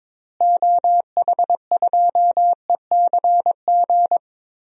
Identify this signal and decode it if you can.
Morse code, heard as a keyed tone, about 22 wpm, 700 Hz. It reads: OH2ECG
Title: DXCC-100 Call Signs - 6 Characters - Repeated 3x - 22wpm